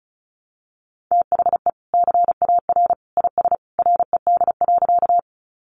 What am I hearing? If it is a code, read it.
Morse code, heard as a keyed tone, about 34 wpm, 700 Hz. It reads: THE CAR IS RED.
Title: Easy Phrases 35wpm